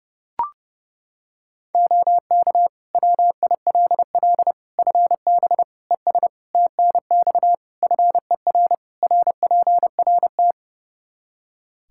Morse code, heard as a keyed tone, about 32 wpm, 700 Hz. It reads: OK WILL FB ES TNX FER RPRT
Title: Practice QSO - Exchange 2 - Info - 30wpm